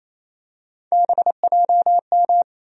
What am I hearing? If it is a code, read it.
Morse code, heard as a keyed tone, about 28 wpm, 700 Hz. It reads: BJM
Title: USA Call Sign Postfixes - 3 Letters 28wpm